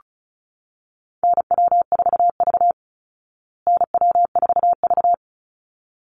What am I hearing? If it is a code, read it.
Morse code, heard as a keyed tone, about 34 wpm, 700 Hz. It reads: NW4V NW4V
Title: Call Signs Repeated 3x - 35wpm